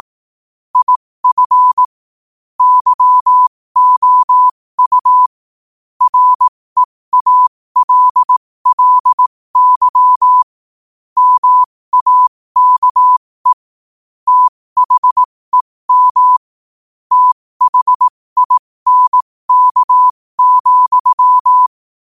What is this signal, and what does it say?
Morse code, heard as a keyed tone, about 18 wpm, spaced as for 15 wpm, 1000 Hz. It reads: IF YOU REALLY MAKE THEM THINK,
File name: qotdfiles / 15wpm